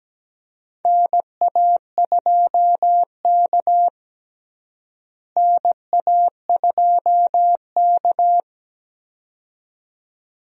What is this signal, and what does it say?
Morse code, heard as a keyed tone, about 17 wpm, 700 Hz. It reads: NA2K NA2K
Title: USA Call Signs - 4 Characters - Repeated 2x - 17wpm